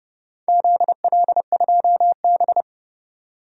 Morse code, heard as a keyed tone, about 30 wpm, 700 Hz. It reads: ZL2B
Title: DXCC-100 Call Signs - 4 Characters - 30wpm